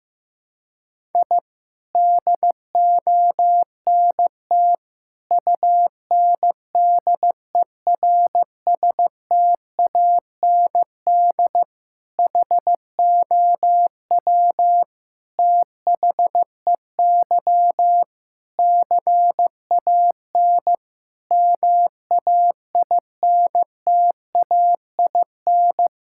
Morse code, heard as a keyed tone, about 15 wpm, 700 Hz. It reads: I DONT UNDERSTAND HOW THEY CAN MAINTAIN
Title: Sentences from Top 1000 Words - Review Part 5 - 15wpm